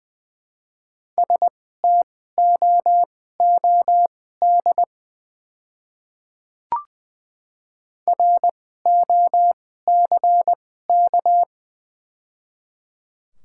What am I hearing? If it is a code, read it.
Morse code, heard as a keyed tone, about 20 wpm, spaced as for 15 wpm, 690 Hz. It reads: STOOD ROCK
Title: 2x Character Spacing - Top 500 Words - 20wpm